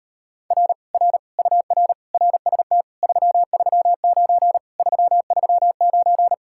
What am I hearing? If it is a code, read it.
Morse code, heard as a keyed tone, about 38 wpm, 700 Hz. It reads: R R UR RST 339 339